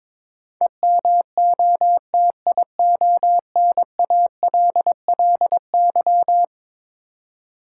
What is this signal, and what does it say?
Morse code, heard as a keyed tone, about 22 wpm, 700 Hz. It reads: EMOTIONALLY
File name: practice / 22wpm / Eleven Letter Words - 22wpm